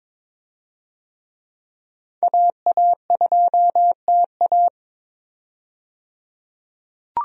Morse code, heard as a keyed tone, about 22 wpm, 700 Hz. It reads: AA2TA
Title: USA Call Signs - 5 Characters - 22wpm